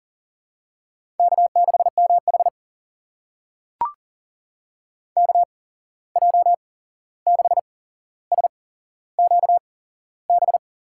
Morse code, heard as a keyed tone, about 39 wpm, spaced as for 15 wpm, 700 Hz. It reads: K6MH KJ6SQB
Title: USA Call Signs - 8x Character Spacing - 40wpm